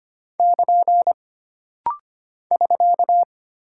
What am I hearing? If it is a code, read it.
Morse code, heard as a keyed tone, about 25 wpm, 700 Hz. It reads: <KN> <SK>
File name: Single Punctuation-Prosign - Rapid-Fire - 25wpm